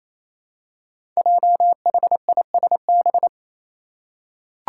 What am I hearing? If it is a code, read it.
Morse code, heard as a keyed tone, about 28 wpm, 700 Hz. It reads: JHISB
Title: Five Challenging Letter-Number Combinations 28wpm